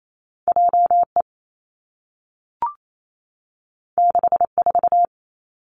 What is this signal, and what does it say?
Morse code, heard as a keyed tone, about 28 wpm, 700 Hz. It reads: J E 6 4
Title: Two Challenging Letter-Number Combinations 28wpm